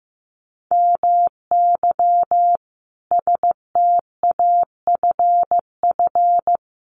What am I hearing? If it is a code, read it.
Morse code, heard as a keyed tone, about 15 wpm, 700 Hz. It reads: MY STAFF
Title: Sentences from Top 1200 Words 15wpm